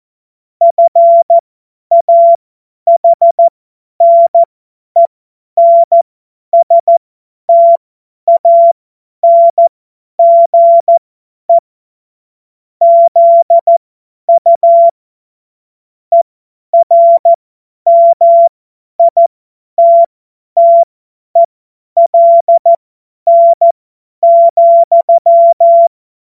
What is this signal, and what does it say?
Morse code, heard as a keyed tone, about 14 wpm, spaced as for 10 wpm, 680 Hz. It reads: FAHNENSTANGE ZU ERMITTELN,
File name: Morsekurs / Witze / Witz 21. Juni 2022 50 F70